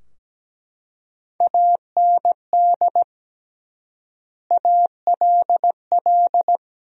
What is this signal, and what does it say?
Morse code, heard as a keyed tone, about 17 wpm, 700 Hz. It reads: AND ALL
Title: 3x Word-Spacing - Sentences from Top 100 Words - 17wpm